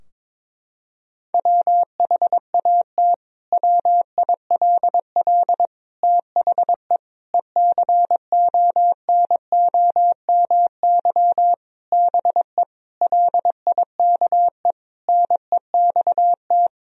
Morse code, heard as a keyed tone, about 22 wpm, 700 Hz. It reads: WHAT WILL THE ECONOMY BE LIKE NEXT
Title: Sentences from Top 1000 Words - Review Part 3 - 22wpm